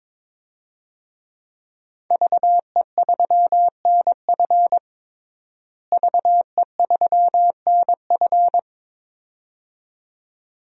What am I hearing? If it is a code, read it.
Morse code, heard as a keyed tone, about 23 wpm, 700 Hz. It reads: VE3NF VE3NF
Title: DXCC-100 Call Signs - 5 Characters - Repeated 2x - 22wpm